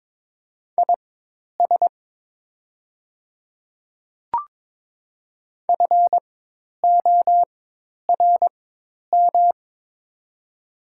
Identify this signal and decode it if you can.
Morse code, heard as a keyed tone, about 22 wpm, spaced as for 10 wpm, 700 Hz. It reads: IS FORM
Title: 4x Character Spacing - Top 200 Words - 22wpm